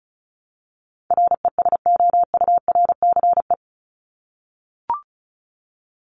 Morse code, heard as a keyed tone, about 34 wpm, 700 Hz. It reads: RESOURCE
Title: Top 1001-1250 Words 35wpm